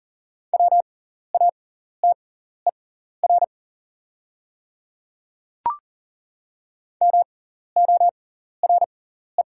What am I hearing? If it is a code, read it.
Morse code, heard as a keyed tone, about 40 wpm, spaced as for 14 wpm, 700 Hz. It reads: WATER MORE
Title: 6x Character Spacing - Top 500 Words - 40wpm